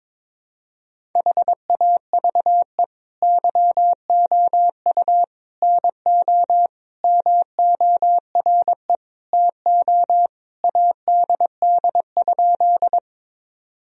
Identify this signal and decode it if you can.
Morse code, heard as a keyed tone, about 22 wpm, 700 Hz. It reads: HAVE YOU NO MORE TO ADD?
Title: Sentences from Top 1000 Words - Review Part 1 - 22wpm